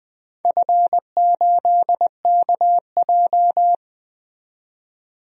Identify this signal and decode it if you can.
Morse code, heard as a keyed tone, about 20 wpm, 700 Hz. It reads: F8KJ
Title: DXCC-100 Call Signs - 4 Characters - 20wpm